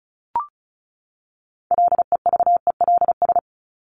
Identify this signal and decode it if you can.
Morse code, heard as a keyed tone, about 34 wpm, 700 Hz. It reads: LEVELS
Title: ES:S Suffix - Top 100 Words with Base Words 35wpm